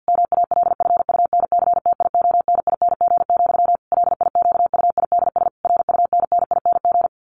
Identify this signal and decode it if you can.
Morse code, heard as a keyed tone, about 50 wpm, 700 Hz. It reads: MALFUNCTIONING, LIQUIDS RUNNING
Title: A Sound Of Thunder 50wpm